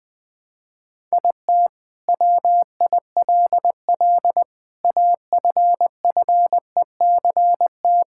Morse code, heard as a keyed tone, about 20 wpm, 700 Hz. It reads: IT WILL AFFECT
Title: Sentences from Top 1000 Words - Review Part 6 - 20wpm